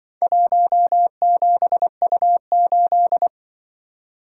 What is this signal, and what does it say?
Morse code, heard as a keyed tone, about 24 wpm, 700 Hz. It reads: 17U8
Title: Speed-Racing - Four Letter-Number Combinations - 20wpm